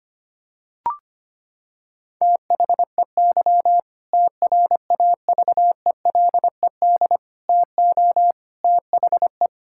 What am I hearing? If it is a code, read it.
Morse code, heard as a keyed tone, about 25 wpm, 700 Hz. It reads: THEY TRAVELED TO THE
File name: practice / 25wpm / Sentences with 2-Word Phrases - 25wpm